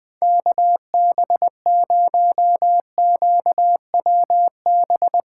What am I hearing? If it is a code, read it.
Morse code, heard as a keyed tone, about 21 wpm, 700 Hz. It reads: KB0QWB
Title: Speed Racing - USA 2x3 Call Signs 17wpm